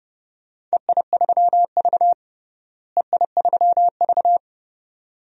Morse code, heard as a keyed tone, about 30 wpm, 700 Hz. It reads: EI3V EI3V
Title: DXCC-100 Call Signs - 4 Characters - Repeated 2x - 30wpm